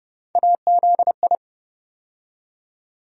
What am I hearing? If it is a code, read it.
Morse code, heard as a keyed tone, about 32 wpm, 700 Hz. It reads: AZI
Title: USA Call Sign Postfixes - 3 Letters 30wpm